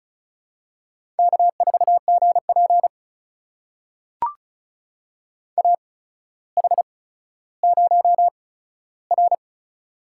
Morse code, heard as a keyed tone, about 36 wpm, spaced as for 15 wpm, 700 Hz. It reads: K4GP AH0R
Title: USA Call Signs - 8x Character Spacing - 35wpm